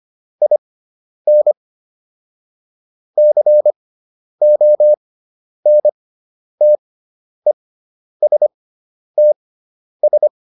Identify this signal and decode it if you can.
Morse code, heard as a keyed tone, about 25 wpm, spaced as for 10 wpm, 600 Hz. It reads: IN CONTESTS